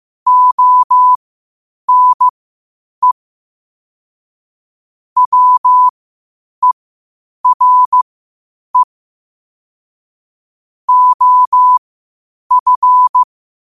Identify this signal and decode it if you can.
Morse code, heard as a keyed tone, about 15 wpm, spaced as for 8 wpm, 1000 Hz. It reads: ONE WERE OF
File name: qotdfiles / 8wpm